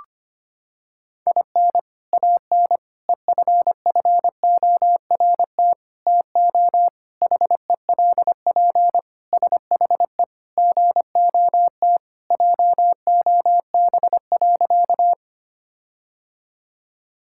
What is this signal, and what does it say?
Morse code, heard as a keyed tone, about 25 wpm, 700 Hz. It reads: IN AN EFFORT TO HELP SHE GOT JOB.
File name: practice / 25wpm / Sentences with 4-Word Phrases - 25wpm